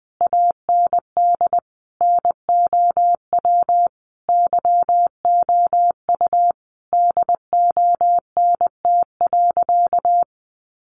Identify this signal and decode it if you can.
Morse code, heard as a keyed tone, about 20 wpm, 700 Hz. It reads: AND NOW YOU DONT.
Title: Sentences from Top 600 Words 20wpm